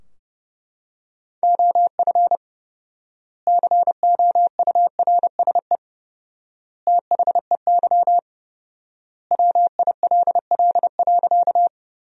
Morse code, heard as a keyed tone, about 30 wpm, 700 Hz. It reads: OF COURSE THEY WILL.
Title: 4x Word-Spacing - Sentences from Top 500 Words - 30wpm